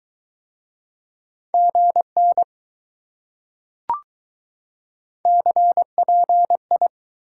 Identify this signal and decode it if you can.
Morse code, heard as a keyed tone, about 21 wpm, 700 Hz. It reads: GN CPI